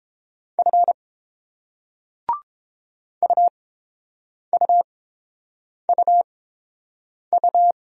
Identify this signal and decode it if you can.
Morse code, heard as a keyed tone, about 30 wpm, spaced as for 7 wpm, 700 Hz. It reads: F UUUU